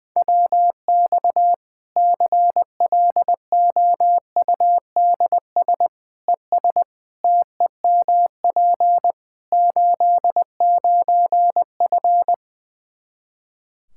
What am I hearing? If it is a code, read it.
Morse code, heard as a keyed tone, about 20 wpm, 700 Hz. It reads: WX CLOUDS ES TEMP 89F
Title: Practice QSO - Exchange 2 - Info - 20wpm